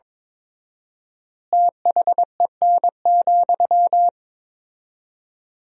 Morse code, heard as a keyed tone, about 22 wpm, 700 Hz. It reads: THEN,
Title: 4x Word-Spacing - Sentences from Top 500 Words - 22wpm